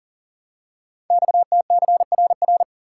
Morse code, heard as a keyed tone, about 42 wpm, 690 Hz. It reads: XTCRR
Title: Code Groups - 5 Letters - 40wpm